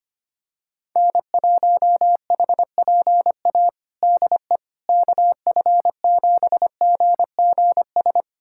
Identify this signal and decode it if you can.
Morse code, heard as a keyed tone, about 25 wpm, 700 Hz. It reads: N1HPA DE KF7GGS
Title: USA Callsign DE Callsign 25wpm